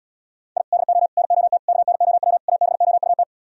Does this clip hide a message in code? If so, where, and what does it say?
Yes: Morse code from 0.6 to 3.2 s, heard as a keyed tone, about 77 wpm, 700 Hz. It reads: I DO NOT KNOW ABOUT